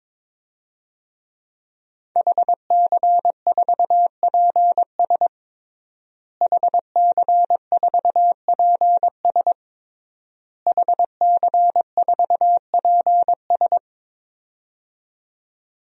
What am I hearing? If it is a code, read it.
Morse code, heard as a keyed tone, about 22 wpm, 700 Hz. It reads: HC4PS HC4PS HC4PS
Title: DXCC-100 Call Signs - 5 Characters - Repeated 3x - 22wpm